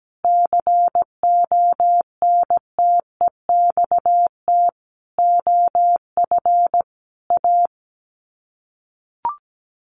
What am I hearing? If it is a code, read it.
Morse code, heard as a keyed tone, about 17 wpm, 700 Hz. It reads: CONTEXT OF A
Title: Sets of 5 Words Encore 17wpm